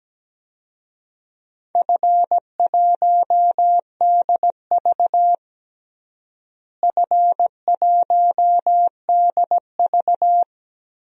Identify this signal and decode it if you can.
Morse code, heard as a keyed tone, about 17 wpm, 700 Hz. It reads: F1DV F1DV
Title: DXCC-100 Call Signs - 4 Characters - Repeated 2x - 17wpm